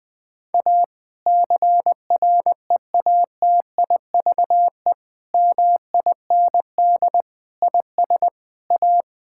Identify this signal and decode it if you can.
Morse code, heard as a keyed tone, about 21 wpm, 700 Hz. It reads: A CREATIVE MIND IS A
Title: Sentences from Top 1000 Words - Review Part 5 - 20wpm